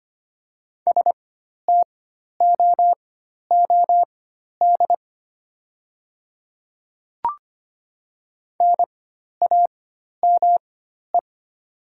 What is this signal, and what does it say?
Morse code, heard as a keyed tone, about 25 wpm, spaced as for 12 wpm, 700 Hz. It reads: STOOD NAME
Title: 4x Character Spacing - Top 500 Words - 25wpm